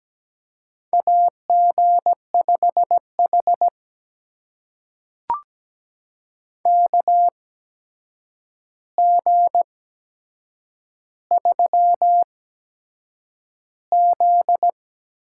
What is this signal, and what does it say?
Morse code, heard as a keyed tone, about 17 wpm, spaced as for 7 wpm, 700 Hz. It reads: AG5H KG3Z
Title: USA Call Signs - 4 Characters - 8x Character Spacing - 17wpm